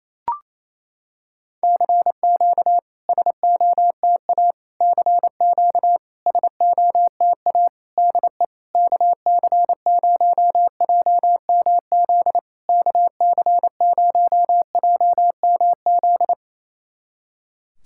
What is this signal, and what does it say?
Morse code, heard as a keyed tone, about 28 wpm, 700 Hz. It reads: CQ SOTA CQ SOTA DE KC0JMZ KC0JMZ
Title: SOTA QSO Practice 28wpm